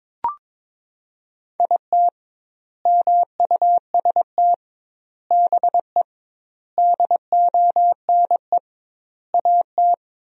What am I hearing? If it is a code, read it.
Morse code, heard as a keyed tone, about 23 wpm, 700 Hz. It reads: IT MUST BE DONE AT
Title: 2x Word-Spacing - Sentences from Top 500 Words - 22wpm